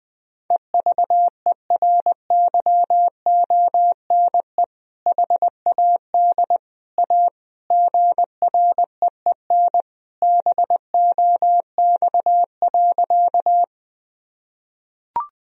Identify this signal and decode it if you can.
Morse code, heard as a keyed tone, about 21 wpm, 700 Hz. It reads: EVERYONE HAD A GREEN BOX.
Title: Sentences from Top 1000 Words - Review Part 2 - 20wpm